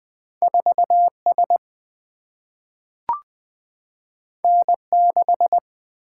Morse code, heard as a keyed tone, about 20 wpm, 700 Hz. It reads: 4S N6
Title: Single-Letter Single-Number 20wpm